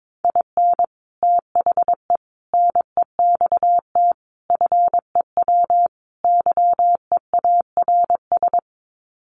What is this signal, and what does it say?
Morse code, heard as a keyed tone, about 22 wpm, 700 Hz. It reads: IN THE NEXT FEW YEARS
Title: Sets of 5 Words Encore 22wpm